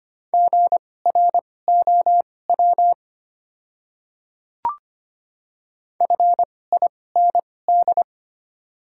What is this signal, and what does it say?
Morse code, heard as a keyed tone, about 25 wpm, spaced as for 18 wpm, 700 Hz. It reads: GROW FIND
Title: 2x Character Spacing - Top 500 Words - 25wpm